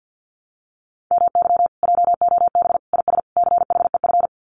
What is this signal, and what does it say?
Morse code, heard as a keyed tone, about 50 wpm, 700 Hz. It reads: MY JOB IS CHEF
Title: Common QSO Phrases Encore 50wpm